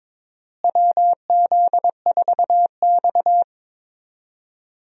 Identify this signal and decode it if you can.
Morse code, heard as a keyed tone, about 22 wpm, 700 Hz. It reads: WZ4X
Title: USA Call Signs - 4 Characters - Repeated 3x - 22wpm